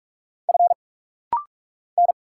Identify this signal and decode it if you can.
Morse code, heard as a keyed tone, about 46 wpm, 700 Hz. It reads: F N
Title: Single Letters - Rapid-Fire 45wpm